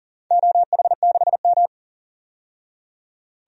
Morse code, heard as a keyed tone, about 40 wpm, 700 Hz. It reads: OH6M
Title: DXCC-100 Call Signs - 4 Characters - 40wpm